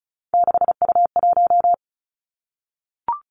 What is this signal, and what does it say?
Morse code, heard as a keyed tone, about 36 wpm, 700 Hz. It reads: 6U1
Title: Three Challenging Letter-Number Combinations 35wpm